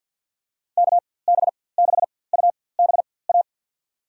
Morse code, heard as a keyed tone, about 47 wpm, spaced as for 23 wpm, 700 Hz. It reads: KB6UBA
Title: USA Call Signs - 6 Characters - 4x Character Spacing - 50wpm